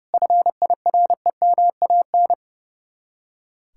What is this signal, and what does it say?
Morse code, heard as a keyed tone, about 39 wpm, 700 Hz. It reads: FIREMAN